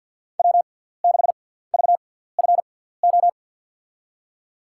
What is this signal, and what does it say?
Morse code, heard as a keyed tone, about 50 wpm, spaced as for 17 wpm, 700 Hz. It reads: W6VFO
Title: USA Call Signs - 5 Characters - 6x Character Spacing - 50wpm